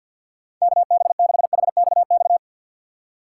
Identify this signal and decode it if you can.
Morse code, heard as a keyed tone, about 46 wpm, 690 Hz. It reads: KB6HXX
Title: USA Call Signs - 6 Characters - 3x Character Spacing - 50wpm